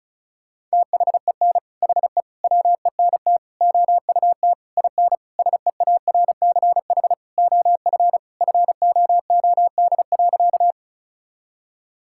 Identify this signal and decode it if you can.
Morse code, heard as a keyed tone, about 36 wpm, 700 Hz. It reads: THEN HE WENT OUT IN SEARCH OF FOOD.
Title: Sentences from Top 1000 Words - Review Part 5 - 35wpm